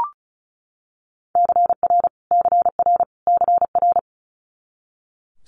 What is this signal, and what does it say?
Morse code, heard as a keyed tone, about 36 wpm, 700 Hz. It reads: CR CR CR
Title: Multi-letter Phonemes Repeated 3x 35wpm